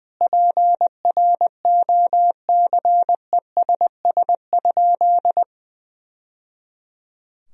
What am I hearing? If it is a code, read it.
Morse code, heard as a keyed tone, about 21 wpm, 700 Hz. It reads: PROCESS?
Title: Sentences from Top 1000 Words - Review Part 3 - 20wpm